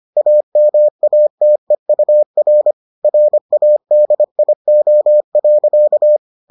Morse code, heard as a keyed tone, about 25 wpm, 600 Hz. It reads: AMATEUR RADIO.